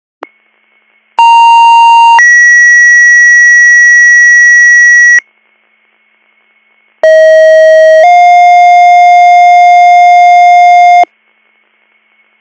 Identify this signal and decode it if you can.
Two-tone page, heard as a one-second tone, then a three-second tone, about 930 then 1950 Hz, about 640 then 710 Hz. It is heard from 1.2 to 11.0 s.